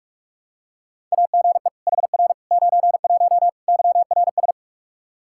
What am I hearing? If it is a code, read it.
Morse code, heard as a keyed tone, about 46 wpm, 700 Hz. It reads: AGE HR 91 YRS
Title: K1USN SST QSO Practice 45wpm